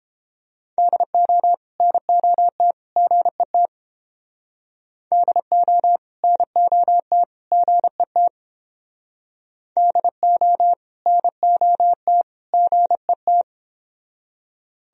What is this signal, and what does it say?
Morse code, heard as a keyed tone, about 29 wpm, 700 Hz. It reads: DO NOT GET DO NOT GET DO NOT GET